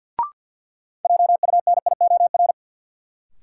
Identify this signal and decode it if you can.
Morse code, heard as a keyed tone, about 53 wpm, 700 Hz. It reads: JUNIOR
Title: Sentences from Top 1500 Words - New Word Drill Alt - 50wpm